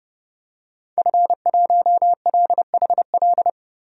Morse code, heard as a keyed tone, about 30 wpm, 700 Hz. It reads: F1LHL
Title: DXCC-100 Call Signs - 5 Characters - 30wpm